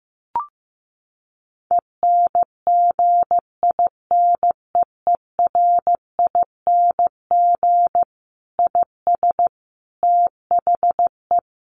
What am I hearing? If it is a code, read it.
Morse code, heard as a keyed tone, about 15 wpm, 700 Hz. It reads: ENGINEERING IS THE
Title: Sentences from Top 1500 Words 15wpm